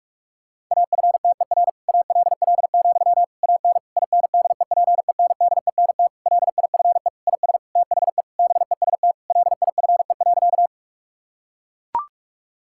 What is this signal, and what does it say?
Morse code, heard as a keyed tone, about 41 wpm, 700 Hz. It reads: AFTER ALL, AN INDEPENDENT LIFE IS THE BEST LIFE.
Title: Sentences from Top 1000 Words - Review Part 6 - 45wpm